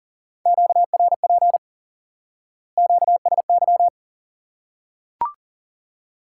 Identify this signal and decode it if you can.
Morse code, heard as a keyed tone, about 40 wpm, 700 Hz. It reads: QRP QSY
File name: IQE Course 3 - Lesson 89 - YAGI - Two Elements - 40wpm